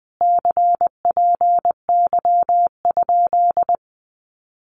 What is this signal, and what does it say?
Morse code, heard as a keyed tone, about 20 wpm, 700 Hz. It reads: CPY?
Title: Common QSO Phrases Encore 20wpm